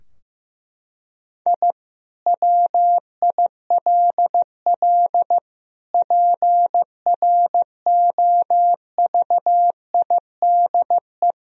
Morse code, heard as a keyed tone, about 15 wpm, 700 Hz. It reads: I WILL PROVIDE
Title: Sentences from Top 1000 Words - Review Part 4 - 15wpm